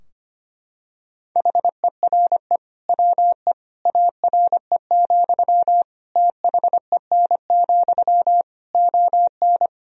Morse code, heard as a keyed tone, about 25 wpm, 700 Hz. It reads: HERE WE ARE, THEN, ON
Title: Sentences from Top 1000 Words - Review Part 3 - 25wpm